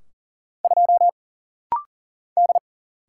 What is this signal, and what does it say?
Morse code, heard as a keyed tone, about 40 wpm, 700 Hz. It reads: 2 D
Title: Single Letter-Number - Rapid-Fire 40wpm